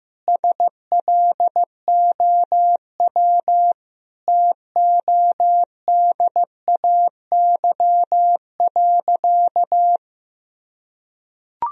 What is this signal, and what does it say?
Morse code, heard as a keyed tone, about 15 wpm, 700 Hz. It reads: SLOW TODAY.
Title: Sentences from Top 1000 Words - Review Part 2 - 15wpm